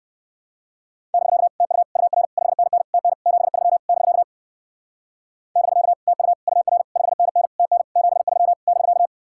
Morse code, heard as a keyed tone, about 58 wpm, 700 Hz. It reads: <BK> TU UR 5NN TN 73 <BK> <BK> TU UR 5NN TN 73 <BK>